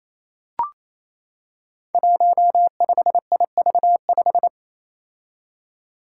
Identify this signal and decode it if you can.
Morse code, heard as a keyed tone, about 29 wpm, 700 Hz. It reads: 15IV5
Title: Five Challenging Letter-Number Combinations 28wpm